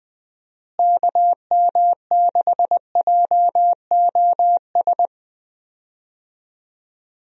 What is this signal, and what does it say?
Morse code, heard as a keyed tone, about 19 wpm, 700 Hz. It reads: KM6JOS
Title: USA Call Signs - 6 Characters - Repeated 3x - 20wpm